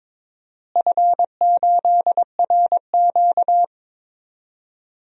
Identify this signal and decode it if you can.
Morse code, heard as a keyed tone, about 22 wpm, 700 Hz. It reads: F8RQ
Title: DXCC-100 Call Signs - 4 Characters - Repeated 3x - 22wpm